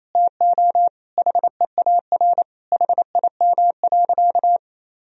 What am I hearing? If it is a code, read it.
Morse code, heard as a keyed tone, about 28 wpm, 700 Hz. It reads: TO HEAR HIM.
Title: Sentences from Top 1000 Words - Review Part 1 - 28wpm